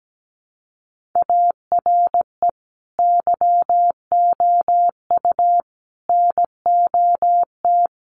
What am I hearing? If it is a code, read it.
Morse code, heard as a keyed tone, about 17 wpm, 700 Hz. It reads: ARE YOU NOT
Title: Sentences from Top 600 Words 17wpm